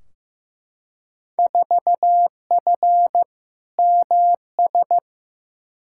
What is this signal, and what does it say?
Morse code, heard as a keyed tone, about 15 wpm, 700 Hz. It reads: 4F MS
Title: Field Day Exchanges Encore - 15wpm